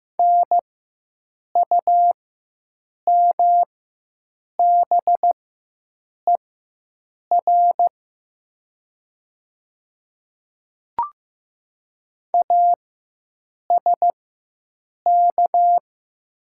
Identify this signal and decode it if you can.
Morse code, heard as a keyed tone, about 15 wpm, spaced as for 7 wpm, 700 Hz. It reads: NUMBER ASK
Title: 4x Character Spacing - Top 300 Words - 15wpm